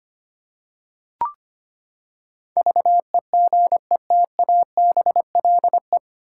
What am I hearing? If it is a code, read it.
Morse code, heard as a keyed tone, about 25 wpm, 700 Hz. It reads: VEGETABLE
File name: Nine Letter Words - 25wpm